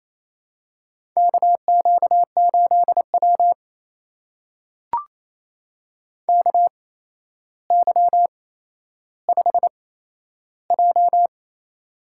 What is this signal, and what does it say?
Morse code, heard as a keyed tone, about 27 wpm, spaced as for 12 wpm, 700 Hz. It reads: KQ8W KY5J
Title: USA Call Signs - 4 Characters - 8x Character Spacing - 28wpm